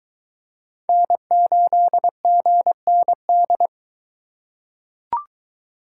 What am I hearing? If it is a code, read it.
Morse code, heard as a keyed tone, about 22 wpm, 700 Hz. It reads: N8GND